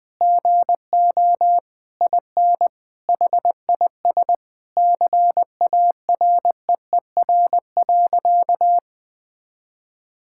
Morse code, heard as a keyed tone, about 20 wpm, 700 Hz. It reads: GO IN HIS CAREER.
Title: Sentences with 7-Word Phrases - Spoken Beginning - 20wpm